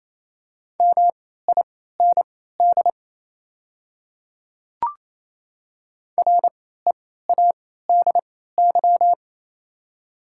Morse code, heard as a keyed tone, about 28 wpm, spaced as for 16 wpm, 700 Hz. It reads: MIND READY
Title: 3x Character Spacing - Top 500 Words - 28wpm